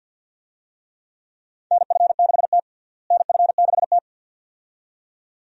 Morse code, heard as a keyed tone, about 52 wpm, 700 Hz. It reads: NF6T NF6T
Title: USA Call Signs - 4 Characters - Repeated 2x - 50wpm